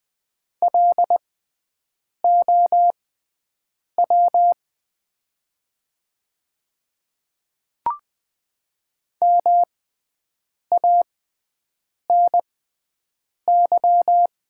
Morse code, heard as a keyed tone, about 20 wpm, spaced as for 7 wpm, 690 Hz. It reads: LOW MANY
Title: 6x Character Spacing - Top 300 Words - 20wpm